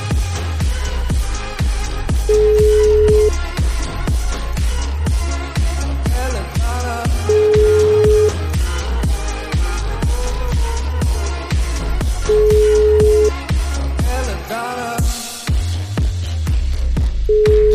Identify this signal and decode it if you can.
Ringback tone (Europe, 425 Hz).